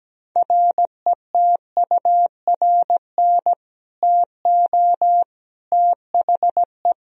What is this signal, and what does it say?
Morse code, heard as a keyed tone, about 17 wpm, 700 Hz. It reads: RETURN TO THE
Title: Sentences from Top 1000 Words - Review Part 2 - 17wpm